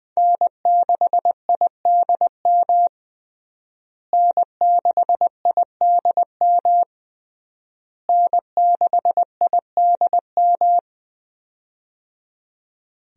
Morse code, heard as a keyed tone, about 20 wpm, 700 Hz. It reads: N6IDM N6IDM N6IDM
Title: USA Call Signs - 5 Characters - Repeated 3x - 20wpm